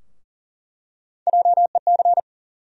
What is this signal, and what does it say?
Morse code, heard as a keyed tone, about 40 wpm, 700 Hz. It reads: JEC
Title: USA Call Sign Postfixes - 3 Letters 40wpm